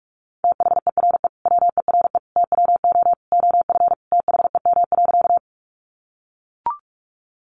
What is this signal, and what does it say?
Morse code, heard as a keyed tone, about 44 wpm, 700 Hz. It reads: THERE WERE TWO OF THEM.
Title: Sentences from Top 100 Words 45wpm